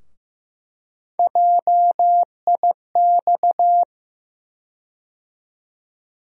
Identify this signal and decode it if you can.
Morse code, heard as a keyed tone, about 15 wpm, 700 Hz. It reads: JIX
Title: USA Call Sign Postfixes - 3 Letters 15wpm